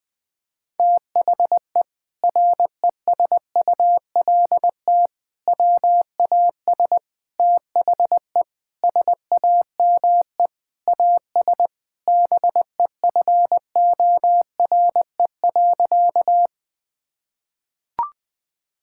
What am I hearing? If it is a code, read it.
Morse code, heard as a keyed tone, about 20 wpm, 700 Hz. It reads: THE RESULT WAS THE SAME AS BEFORE.
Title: Sentences from Top 1000 Words - Review Part 5 - 20wpm